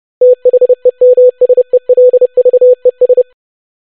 Morse code, heard as a keyed tone, about 30 wpm, 500 Hz. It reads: THEMSELVES
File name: basic500-25wpm